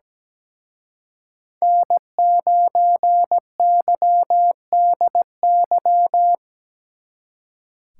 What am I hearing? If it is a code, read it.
Morse code, heard as a keyed tone, about 20 wpm, 700 Hz. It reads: N9YDY
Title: Speed Racing - USA 1x3 Call Signs 17wpm